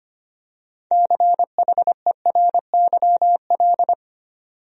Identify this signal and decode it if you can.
Morse code, heard as a keyed tone, about 25 wpm, 700 Hz. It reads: CHERYL
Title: USA Common Names - 25wpm